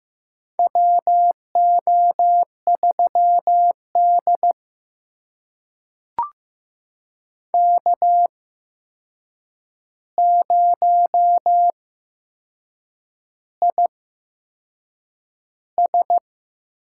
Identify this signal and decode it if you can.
Morse code, heard as a keyed tone, about 15 wpm, spaced as for 6 wpm, 700 Hz. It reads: WO3D K0IS
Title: USA Call Signs - 4 Characters - 8x Character Spacing - 15wpm